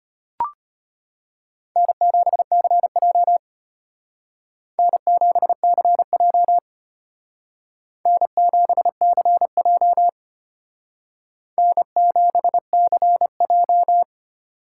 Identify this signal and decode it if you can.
Morse code, heard as a keyed tone, about 30 wpm, 700 Hz. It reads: N7CJ N7CJ N7CJ N7CJ
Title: Speed Racing - USA 1x2 Call Signs 25wpm